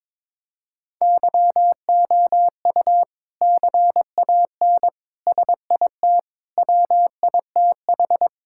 Morse code, heard as a keyed tone, about 22 wpm, 700 Hz. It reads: YOU CAN SIT WITH
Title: Sentences from Top 1000 Words - Review Part 5 - 22wpm